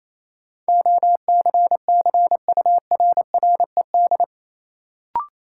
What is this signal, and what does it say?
Morse code, heard as a keyed tone, about 28 wpm, 700 Hz. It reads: OCCURRED
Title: Sets of 5 Words Encore - New Word Drill - 28wpm